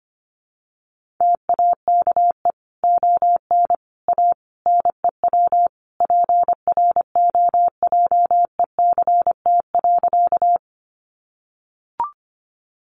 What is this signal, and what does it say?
Morse code, heard as a keyed tone, about 25 wpm, 700 Hz. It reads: TAKE ON A NEW PROJECT.
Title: Sentences from Top 1400 Words 25wpm